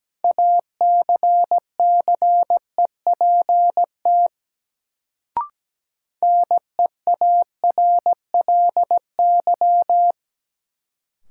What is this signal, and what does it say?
Morse code, heard as a keyed tone, about 17 wpm, 700 Hz. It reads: ACCEPT NEARLY
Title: Sentences from Top 1300 Words - New Word Drill Alt - 17wpm